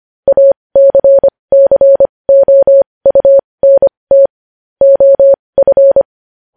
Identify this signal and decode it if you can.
Morse code, heard as a keyed tone, about 25 wpm, spaced as for 20 wpm, 550 Hz. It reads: ACCOUNT OF